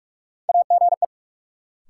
Morse code, heard as a keyed tone, about 45 wpm, 700 Hz. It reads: AGE
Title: Sets of 3 Words - New Word Drill - 45wpm